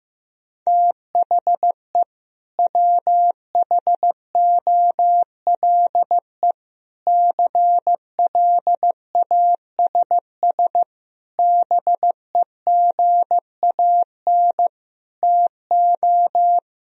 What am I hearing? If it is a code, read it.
Morse code, heard as a keyed tone, about 15 wpm, 700 Hz. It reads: THE WHOLE CLASS BEGAN TO
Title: Sentences from Top 1000 Words - Review Part 2 - 15wpm